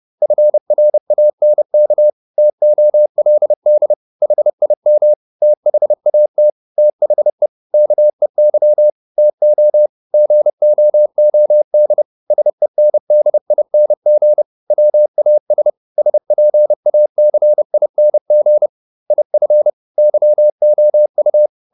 Morse code, heard as a keyed tone, about 30 wpm, 600 Hz. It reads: FRANK TOLD HIM THAT THE KEY TO GOOD SENDING WAS SPACING IF YOU